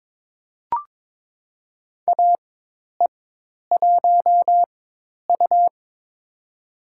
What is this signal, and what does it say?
Morse code, heard as a keyed tone, about 22 wpm, spaced as for 10 wpm, 700 Hz. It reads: AE1U
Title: USA Call Signs - 4x Character Spacing - 22wpm